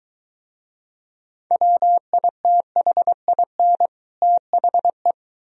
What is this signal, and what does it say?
Morse code, heard as a keyed tone, about 23 wpm, 700 Hz. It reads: WITHIN THE